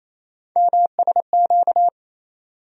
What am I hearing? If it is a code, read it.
Morse code, heard as a keyed tone, about 28 wpm, 700 Hz. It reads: MSQ
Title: USA Call Sign Postfixes - 3 Letters 28wpm